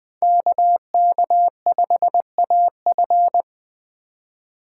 Morse code, heard as a keyed tone, about 20 wpm, 700 Hz. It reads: KK5AF